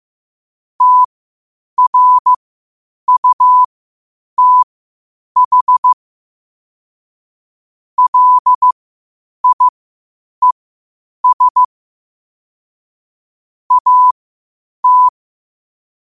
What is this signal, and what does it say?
Morse code, heard as a keyed tone, about 15 wpm, spaced as for 8 wpm, 1000 Hz. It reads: TRUTH LIES AT
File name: qotdfiles / 8wpm